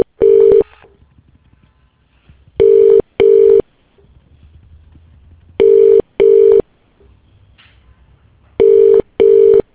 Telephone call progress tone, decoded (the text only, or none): ringback tone